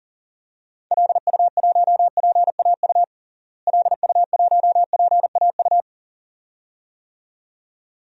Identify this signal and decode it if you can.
Morse code, heard as a keyed tone, about 40 wpm, 700 Hz. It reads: LU1PAU LU1PAU
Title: DXCC-100 Call Signs - 6 Characters - Repeated 2x - 40wpm